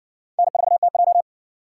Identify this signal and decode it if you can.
Morse code, heard as a keyed tone, about 60 wpm, 700 Hz. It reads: N4TJ